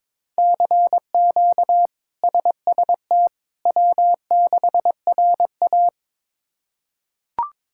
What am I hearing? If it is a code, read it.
Morse code, heard as a keyed tone, about 22 wpm, 700 Hz. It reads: CQ SST W6RA
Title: K1USN SST QSO Practice - 22wpm